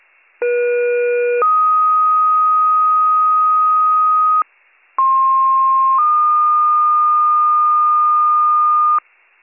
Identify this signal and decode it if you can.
Two-tone page, heard as a one-second tone, then a three-second tone, about 480 then 1250 Hz, about 1060 then 1220 Hz. It is heard from 0.4 to 9.0 s.